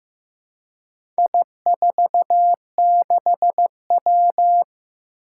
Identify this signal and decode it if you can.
Morse code, heard as a keyed tone, about 16 wpm, 700 Hz. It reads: I46W
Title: Four Challenging Letter-Number Combinations 15wpm